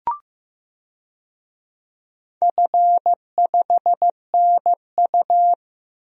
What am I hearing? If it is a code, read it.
Morse code, heard as a keyed tone, about 15 wpm, 700 Hz. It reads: F5NU
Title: DXCC-100 Call Signs - 4 Characters - 15wpm